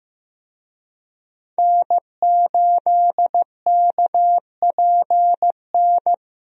Morse code, heard as a keyed tone, about 15 wpm, 700 Hz. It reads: N8KPN
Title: USA Call Signs - 5 Characters - Repeated 3x - 15wpm